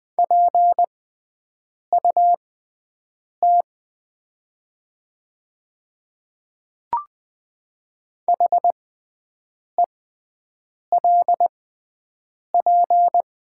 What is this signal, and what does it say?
Morse code, heard as a keyed tone, about 20 wpm, spaced as for 7 wpm, 700 Hz. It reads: PUT HELP
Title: 6x Character Spacing - Top 300 Words - 20wpm